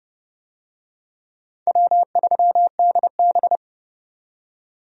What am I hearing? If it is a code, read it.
Morse code, heard as a keyed tone, about 29 wpm, 700 Hz. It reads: W3DB
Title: USA Call Signs - 4 Characters - 30wpm